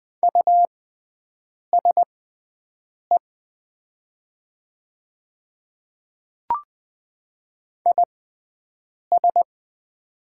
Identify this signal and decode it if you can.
Morse code, heard as a keyed tone, about 20 wpm, spaced as for 7 wpm, 700 Hz. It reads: USE IS
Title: 6x Character Spacing - Top 100 Words - 20wpm